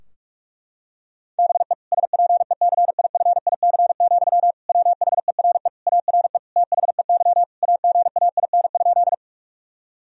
Morse code, heard as a keyed tone, about 46 wpm, 700 Hz. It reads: BE SPECIFIC, WHERE ARE THEY AGAIN?
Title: Sentences from Top 1000 Words - Review Part 3 - 45wpm